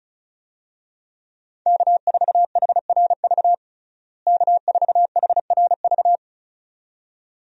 Morse code, heard as a keyed tone, about 36 wpm, 700 Hz. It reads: K4HRV K4HRV
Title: USA Call Signs - 5 Characters - Repeated 2x - 35wpm